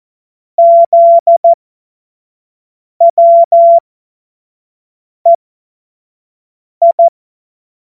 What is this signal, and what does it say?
Morse code, heard as a keyed tone, about 14 wpm, spaced as for 5 wpm, 680 Hz. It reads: ZWEI
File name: Morsekurs / Witze / Witz 08. Februar 2022 25 F70